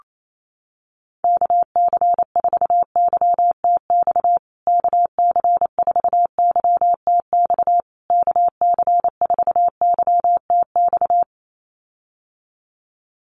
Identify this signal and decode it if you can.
Morse code, heard as a keyed tone, about 28 wpm, 700 Hz. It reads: KC4YTX KC4YTX KC4YTX
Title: USA Call Signs Repeated 3x 28wpm